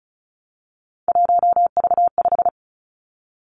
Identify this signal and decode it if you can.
Morse code, heard as a keyed tone, about 36 wpm, 700 Hz. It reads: 1V5
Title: Three Challenging Letter-Number Combinations 35wpm